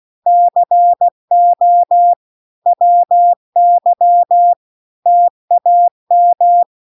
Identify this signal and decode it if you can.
Morse code, heard as a keyed tone, about 16 wpm, 700 Hz. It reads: CO WY TAM
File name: recordings / 16wpm0000